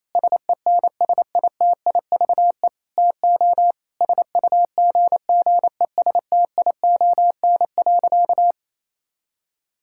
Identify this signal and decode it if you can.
Morse code, heard as a keyed tone, about 28 wpm, 700 Hz. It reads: SENSITIVE TO SUGGESTION.
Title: Sentences from Top 1600 Words 28wpm